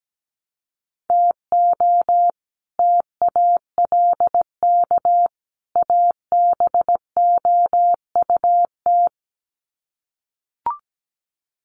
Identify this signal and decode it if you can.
Morse code, heard as a keyed tone, about 17 wpm, 700 Hz. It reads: TO TALK ABOUT
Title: Sets of 3 Words 17wpm